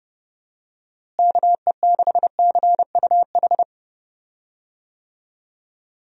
Morse code, heard as a keyed tone, about 32 wpm, 700 Hz. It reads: KE6CUH
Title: USA Call Signs - 6 Characters - Repeated 3x - 30wpm